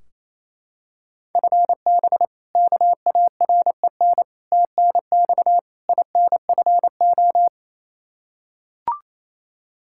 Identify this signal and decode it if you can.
Morse code, heard as a keyed tone, about 28 wpm, 700 Hz. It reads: FB KAREN TNX INFO
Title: K1USN SST QSO Practice 28wpm